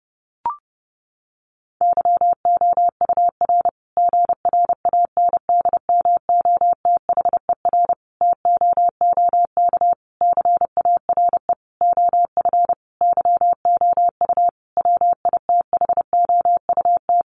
Morse code, heard as a keyed tone, about 30 wpm, 700 Hz. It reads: YOUR GRANDMOTHER TOOK CARE OF YOU WITHOUT
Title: Sentences from Top 1200 Words 30wpm